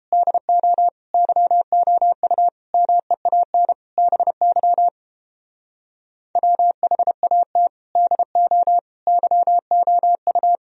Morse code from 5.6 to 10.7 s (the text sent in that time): WHAT DO YOU